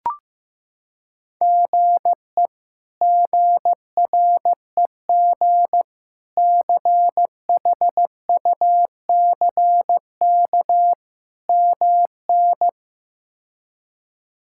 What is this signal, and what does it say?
Morse code, heard as a keyed tone, about 16 wpm, 700 Hz. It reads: GE GREG CHUCK MN
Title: K1USN SST QSO Practice - 15wpm